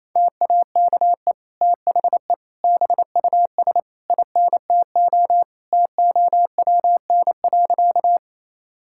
Morse code, heard as a keyed tone, about 28 wpm, 700 Hz. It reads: TAKE THE BUS INTO TOWN.
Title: Sentences from Top 1000 Words - Review Part 6 - 28wpm